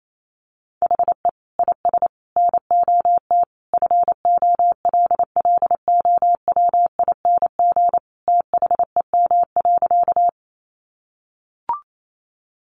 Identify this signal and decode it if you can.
Morse code, heard as a keyed tone, about 28 wpm, 700 Hz. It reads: HE IS NOT FOLLOWING THEM.
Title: Sentences from Top 700 Words 28wpm